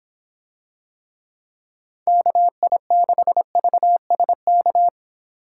Morse code, heard as a keyed tone, about 24 wpm, 700 Hz. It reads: KI6VSK